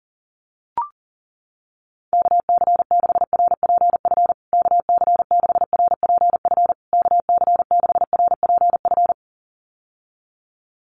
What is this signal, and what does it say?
Morse code, heard as a keyed tone, about 40 wpm, 700 Hz. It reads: KC6RPF KC6RPF KC6RPF
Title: USA Call Signs Repeated 3x 40wpm